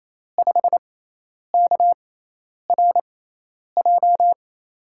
Morse code, heard as a keyed tone, about 28 wpm, spaced as for 10 wpm, 700 Hz. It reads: 5KRJ
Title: USA Call Signs - 6 Characters - 6x Character Spacing - 28wpm